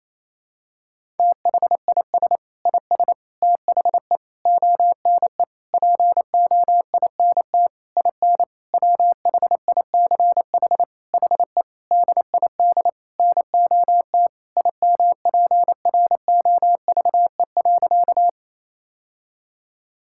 Morse code, heard as a keyed tone, about 28 wpm, 700 Hz. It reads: THIS IS THE ONE POINT IN WHICH HE DID NOT IMPROVE.
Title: Sentences from Top 1000 Words - Review Part 5 - 28wpm